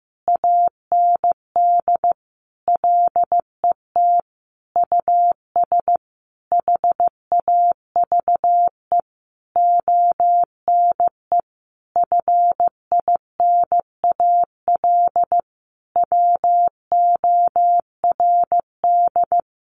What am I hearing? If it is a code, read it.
Morse code, heard as a keyed tone, about 15 wpm, 700 Hz. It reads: AND LET US HAVE ONE FINAL WORD
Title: Sentences from Top 1000 Words 15wpm